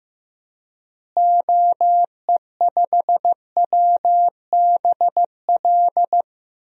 Morse code, heard as a keyed tone, about 15 wpm, 700 Hz. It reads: OE5WBL
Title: DXCC-100 Call Signs - 6 Characters - Repeated 2x - 15wpm